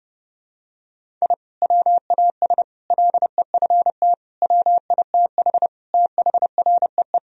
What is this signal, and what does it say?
Morse code, heard as a keyed tone, about 32 wpm, 700 Hz. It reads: I WAS LEFT WITH THREE
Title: Sentences from Top 1000 Words - Review Part 1 - 30wpm